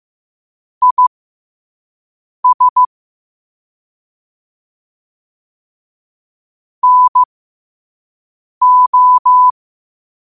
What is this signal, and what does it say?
Morse code, heard as a keyed tone, about 15 wpm, spaced as for 5 wpm, 1000 Hz. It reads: IS NO
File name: qotdfiles / 5wpm